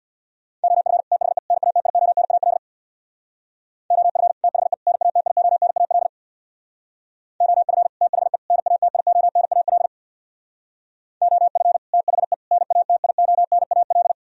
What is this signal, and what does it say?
Morse code, heard as a keyed tone, about 60 wpm, 700 Hz. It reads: OF THE NATIONAL OF THE NATIONAL OF THE NATIONAL OF THE NATIONAL
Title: Speed-Racing - Sets of 3 Words 50wpm